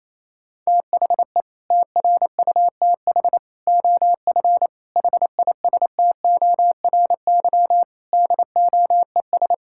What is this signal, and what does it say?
Morse code, heard as a keyed tone, about 28 wpm, 700 Hz. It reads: THE TRUTH OF HISTORY DOES
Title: Sentences from Top 1600 Words 28wpm